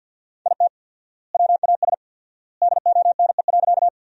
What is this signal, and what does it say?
Morse code, heard as a keyed tone, about 52 wpm, 700 Hz. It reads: IT WAS DONE.
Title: 4x Word-Spacing - Sentences from Top 500 Words - 50wpm